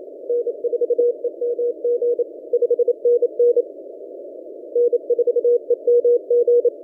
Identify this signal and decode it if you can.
Morse code, heard as a keyed tone, about 27 wpm, 480 Hz. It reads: N4EMG 5NN N4EMG
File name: posts / v63mj-17cw- -0006z